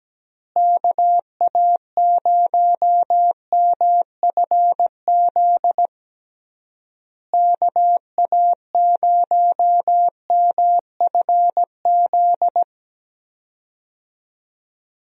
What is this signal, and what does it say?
Morse code, heard as a keyed tone, about 17 wpm, 700 Hz. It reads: KA0MFZ KA0MFZ
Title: USA Call Signs - 6 Characters - Repeated 3x - 17wpm